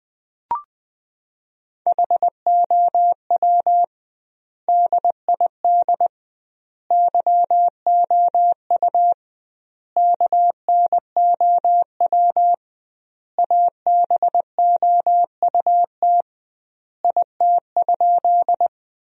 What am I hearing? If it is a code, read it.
Morse code, heard as a keyed tone, about 20 wpm, 700 Hz. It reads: HOW DID YOU KNOW ABOUT IT?
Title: 2x Word-Spacing - Sentences from Top 100 Words - 20wpm